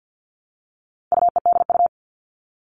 Morse code, heard as a keyed tone, about 50 wpm, 700 Hz. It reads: UEDU
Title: All Cut Numbers - 4-Digit 50wpm